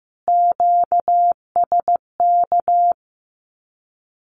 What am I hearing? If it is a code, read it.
Morse code, heard as a keyed tone, about 14 wpm, 700 Hz. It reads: QSK
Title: Q-Codes Repeated 3x 15wpm